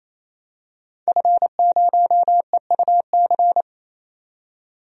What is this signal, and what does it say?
Morse code, heard as a keyed tone, about 28 wpm, 700 Hz. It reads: F0EUC
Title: DXCC-100 Call Signs - 5 Characters - 28wpm